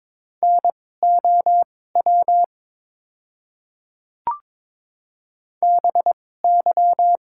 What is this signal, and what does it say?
Morse code, heard as a keyed tone, about 22 wpm, spaced as for 16 wpm, 700 Hz. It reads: NOW BY
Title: 2x Character Spacing - Top 100 Words - 22wpm